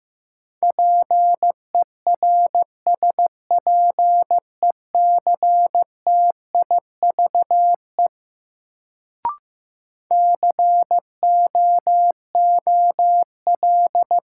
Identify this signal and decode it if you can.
Morse code, heard as a keyed tone, about 15 wpm, 700 Hz. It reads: PERSPECTIVE COOL
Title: Sentences from Top 1200 Words - New Word Drill - 15wpm